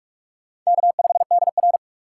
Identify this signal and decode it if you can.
Morse code, heard as a keyed tone, about 47 wpm, 700 Hz. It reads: K5DR
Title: USA Call Signs - 4 Characters - 6x Character Spacing - 45wpm